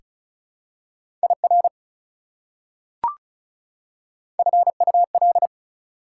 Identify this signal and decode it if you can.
Morse code, heard as a keyed tone, about 36 wpm, 700 Hz. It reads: IR FUL
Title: Prefixes and Suffixes 35wpm